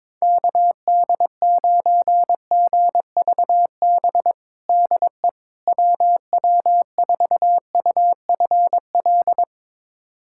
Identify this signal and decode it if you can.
Morse code, heard as a keyed tone, about 21 wpm, 700 Hz. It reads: KD9GVB DE WW4UFL
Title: USA Callsign DE Callsign - 6 Characters - 22wpm